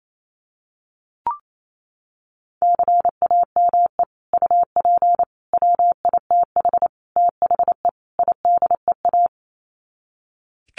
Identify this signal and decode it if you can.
Morse code, heard as a keyed tone, about 28 wpm, 700 Hz. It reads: CAME UP WITH THE IDEA
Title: Sets of 5 Words Encore2 28wpm